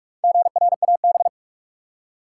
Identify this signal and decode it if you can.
Morse code, heard as a keyed tone, about 46 wpm, 700 Hz. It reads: GRAB
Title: Four Letter Words - 45wpm